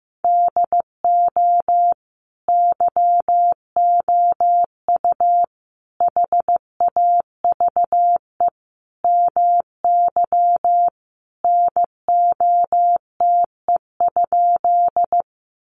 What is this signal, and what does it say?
Morse code, heard as a keyed tone, about 15 wpm, 700 Hz. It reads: DO YOU HAVE MY NOTE?
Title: Sentences from Top 500 Words 15wpm